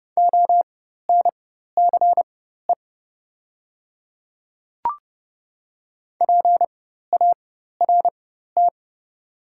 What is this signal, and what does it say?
Morse code, heard as a keyed tone, about 28 wpm, spaced as for 14 wpm, 700 Hz. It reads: ONCE PART
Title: 4x Character Spacing - Top 300 Words - 30wpm